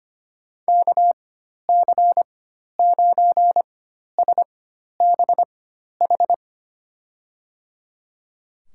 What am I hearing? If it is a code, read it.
Morse code, heard as a keyed tone, about 25 wpm, spaced as for 12 wpm, 700 Hz. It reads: KC9SBH
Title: USA Call Signs - 6 Characters - 4x Character Spacing - 25wpm